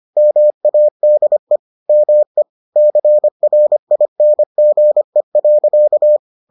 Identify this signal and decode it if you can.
Morse code, heard as a keyed tone, about 25 wpm, 600 Hz. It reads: MADE ME CRINGE.